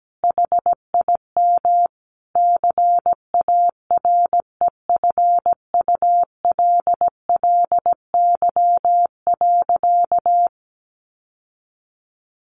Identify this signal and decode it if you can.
Morse code, heard as a keyed tone, about 17 wpm, 700 Hz. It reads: HIM CAREFULLY.
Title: Sentences from Top 1500 Words 17wpm